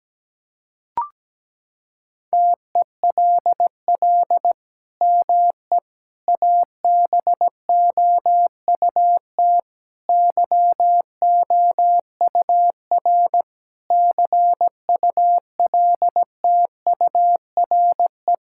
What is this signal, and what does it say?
Morse code, heard as a keyed tone, about 17 wpm, 700 Hz. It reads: TELL ME ABOUT YOUR CULTURE
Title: Sentences from Top 1000 Words - Review Part 6 - 17wpm